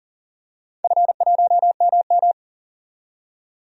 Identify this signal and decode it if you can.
Morse code, heard as a keyed tone, about 40 wpm, 700 Hz. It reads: F1MM
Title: DXCC-100 Call Signs - 4 Characters - 40wpm